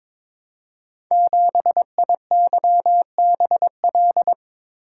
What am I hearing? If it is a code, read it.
Morse code, heard as a keyed tone, about 22 wpm, 700 Hz. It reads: 7IYBL
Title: Code Groups - 5 Letter:Numbers - 22wpm